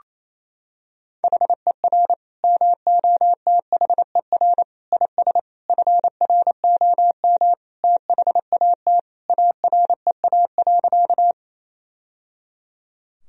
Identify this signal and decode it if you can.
Morse code, heard as a keyed tone, about 28 wpm, 700 Hz. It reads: HER MOTHER IS FROM THAT AREA.
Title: Sentences from Top 1000 Words - Review Part 2 - 28wpm